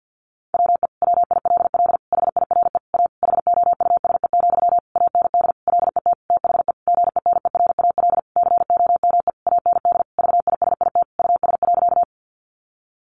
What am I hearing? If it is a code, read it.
Morse code, heard as a keyed tone, about 48 wpm, 700 Hz. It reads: WE WILL HIRE A HOUSE, AND LET THE GENERAL COME AND VISIT US.
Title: Sentences from Top 1400 Words 50wpm